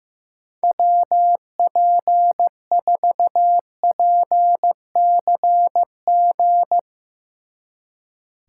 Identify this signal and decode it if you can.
Morse code, heard as a keyed tone, about 15 wpm, 700 Hz. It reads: WP4PCG
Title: Speed Racing - USA 2x3 Call Signs 15wpm